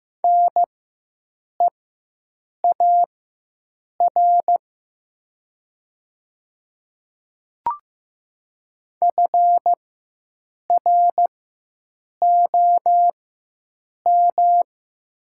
Morse code, heard as a keyed tone, about 15 wpm, spaced as for 7 wpm, 700 Hz. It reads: NEAR FROM
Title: 4x Character Spacing - Top 500 Words - 15wpm